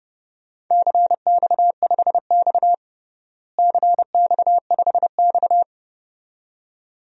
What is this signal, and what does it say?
Morse code, heard as a keyed tone, about 30 wpm, 700 Hz. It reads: CX5X CX5X
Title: DXCC-100 Call Signs - 4 Characters - Repeated 2x - 30wpm